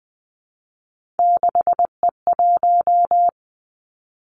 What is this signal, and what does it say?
Morse code, heard as a keyed tone, about 20 wpm, 700 Hz. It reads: 6E1
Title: Three Challenging Letter-Number Combinations 20wpm